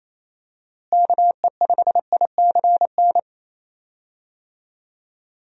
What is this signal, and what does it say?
Morse code, heard as a keyed tone, about 28 wpm, 700 Hz. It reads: KE5ICN
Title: USA Call Signs - 6 Characters - Repeated 2x - 28wpm